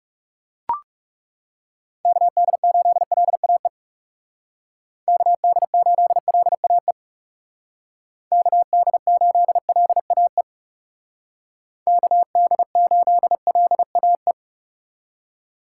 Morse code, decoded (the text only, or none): KD8LAE KD8LAE KD8LAE KD8LAE